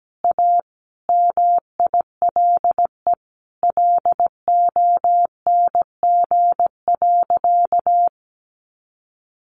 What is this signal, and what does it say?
Morse code, heard as a keyed tone, about 17 wpm, 700 Hz. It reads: A MILE LONG.
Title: Sentences from Top 1300 Words 17wpm